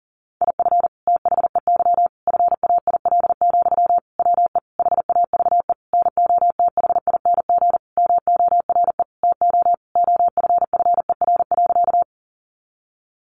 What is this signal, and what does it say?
Morse code, heard as a keyed tone, about 40 wpm, 700 Hz. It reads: IF THEY FAIL, WE HAVE NOTHING MORE TO OFFER.
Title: Sentences from Top 1100 Words 40wpm